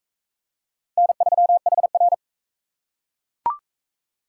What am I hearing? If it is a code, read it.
Morse code, heard as a keyed tone, about 41 wpm, 700 Hz. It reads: N3HR